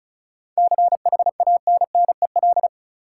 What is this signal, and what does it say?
Morse code, heard as a keyed tone, about 36 wpm, 700 Hz. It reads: CHANNEL
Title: Seven Letter Words - 35wpm